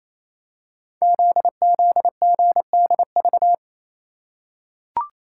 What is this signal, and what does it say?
Morse code, heard as a keyed tone, about 27 wpm, 700 Hz. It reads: ZZGDV
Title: Code Groups - 5 Letters - 28wpm